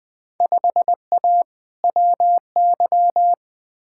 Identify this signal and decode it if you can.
Morse code, heard as a keyed tone, about 20 wpm, 700 Hz. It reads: 5A WY
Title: Field Day Exchanges Encore - 20wpm